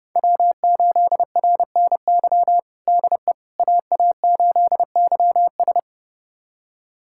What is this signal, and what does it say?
Morse code, heard as a keyed tone, about 30 wpm, 700 Hz. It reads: W8RNY DE AA8YS
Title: USA Callsign DE Callsign 30wpm